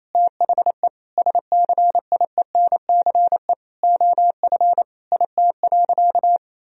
Morse code, heard as a keyed tone, about 28 wpm, 700 Hz. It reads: THE SCIENCE OF IT.
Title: Sentences from Top 1000 Words - Review Part 2 - 28wpm